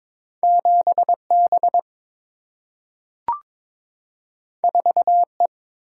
Morse code, heard as a keyed tone, about 22 wpm, 700 Hz. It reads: 7B 4E
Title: Single-Letter Single-Number 22wpm